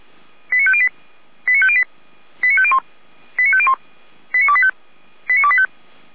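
Five-tone selective call (ZVEI), heard as quick runs of five tones, tones of about 80 ms. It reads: 89508 89508 89501 89501 89285 89285
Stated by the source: ZVEI five-tone selective call 89338, received by SDR.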